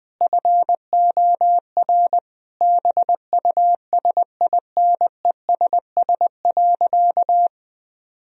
Morse code, heard as a keyed tone, about 20 wpm, 700 Hz. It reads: FOR BUSINESS.
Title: Sentences from Top 1000 Words - Review Part 3 - 20wpm